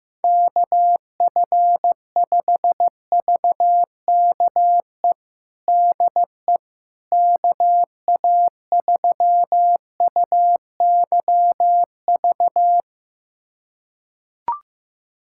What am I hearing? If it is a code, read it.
Morse code, heard as a keyed tone, about 16 wpm, 700 Hz. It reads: KF5VKE DE KA3UYV
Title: USA Callsign DE Callsign - 6 Characters - 15wpm